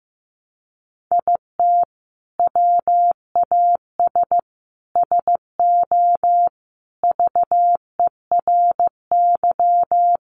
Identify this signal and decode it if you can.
Morse code, heard as a keyed tone, about 15 wpm, 700 Hz. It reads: IT WAS SO VERY
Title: Sentences from Top 500 Words 15wpm